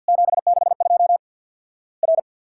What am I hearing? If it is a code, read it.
Morse code, heard as a keyed tone, about 50 wpm, 700 Hz. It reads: 762 R
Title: Worked All Europe DX Contest - QTC Exchange - 50wpm